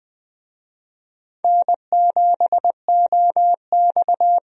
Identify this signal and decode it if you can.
Morse code, heard as a keyed tone, about 20 wpm, 700 Hz. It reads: N7OX
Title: USA Call Signs - 4 Characters - 20wpm